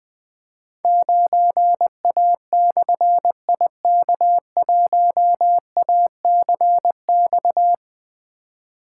Morse code, heard as a keyed tone, about 20 wpm, 700 Hz. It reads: 9A/IK1ACX
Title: Difficult Call Signs 20wpm